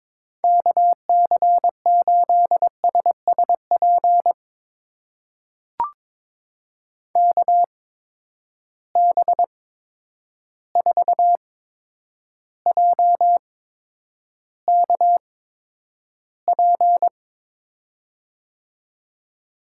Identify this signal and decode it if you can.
Morse code, heard as a keyed tone, about 22 wpm, spaced as for 9 wpm, 700 Hz. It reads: KC8SSP KB4JKP
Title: USA Call Signs - 6 Characters - 8x Character Spacing - 22wpm